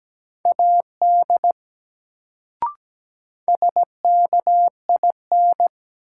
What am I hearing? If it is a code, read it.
Morse code, heard as a keyed tone, about 17 wpm, 700 Hz. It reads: AD SKIN
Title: Sentences from Top 1300 Words - New Word Drill Alt - 17wpm